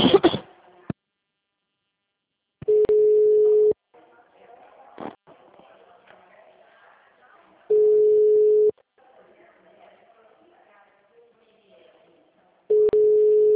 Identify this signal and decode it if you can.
Ringback tone (Europe, 425 Hz), with speech.